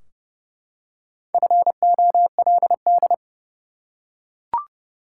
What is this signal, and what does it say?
Morse code, heard as a keyed tone, about 32 wpm, 700 Hz. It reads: FOLD
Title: Four Letter Words - 30wpm